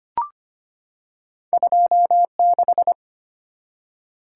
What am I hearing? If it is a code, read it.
Morse code, heard as a keyed tone, about 25 wpm, 700 Hz. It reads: 26
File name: Two Letter-Number Combinations - 25wpm